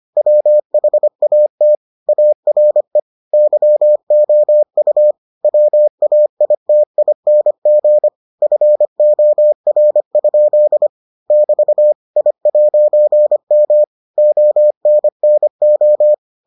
Morse code, heard as a keyed tone, about 25 wpm, 600 Hz. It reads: WHAT ARE YOU WAITING FOR? <BT> I'M ONNO